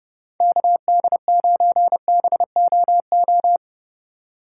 Morse code, heard as a keyed tone, about 31 wpm, 700 Hz. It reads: KD9BOO
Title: USA Call Signs - 6 Characters - 6x Character Spacing - 30wpm